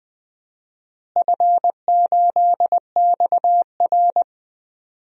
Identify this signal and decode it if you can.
Morse code, heard as a keyed tone, about 20 wpm, 700 Hz. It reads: F8XR
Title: DXCC-100 Call Signs - 4 Characters - Repeated 2x - 20wpm